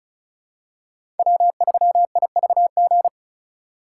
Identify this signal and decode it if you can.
Morse code, heard as a keyed tone, about 36 wpm, 700 Hz. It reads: W3IVG
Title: USA Call Signs - 5 Characters - 2x Character Spacing - 35wpm